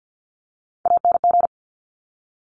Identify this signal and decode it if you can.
Morse code, heard as a keyed tone, about 50 wpm, 700 Hz. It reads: ANG
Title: Top Three Letter Combinations in Words 50wpm